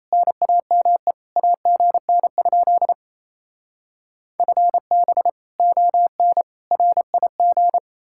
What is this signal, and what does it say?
Morse code, heard as a keyed tone, about 36 wpm, 700 Hz. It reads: NAME AGN? FB ON RIG